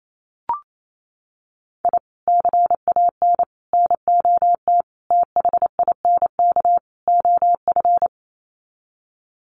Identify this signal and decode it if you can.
Morse code, heard as a keyed tone, about 28 wpm, 700 Hz. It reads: I CAN NOT THINK OF
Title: Sets of 5 Words 28wpm